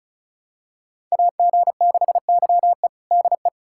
Morse code, heard as a keyed tone, about 36 wpm, 700 Hz. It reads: AG6YE DE
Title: USA Callsign DE Callsign 35wpm